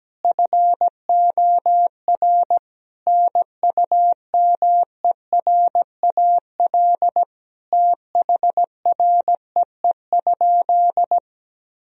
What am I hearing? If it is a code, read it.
Morse code, heard as a keyed tone, about 17 wpm, 700 Hz. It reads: FOR NUMERAL THREE?
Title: Sentences from Top 1000 Words - Review Part 2 - 17wpm